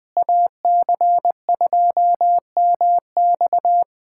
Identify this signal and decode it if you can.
Morse code, heard as a keyed tone, about 20 wpm, 700 Hz. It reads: AC2MX
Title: USA Callsign DE Callsign - 5 Characters - 20wpm